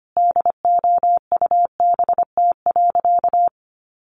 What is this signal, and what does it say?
Morse code, heard as a keyed tone, about 25 wpm, 700 Hz. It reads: DOUBT.
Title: Sentences from Top 1500 Words 25wpm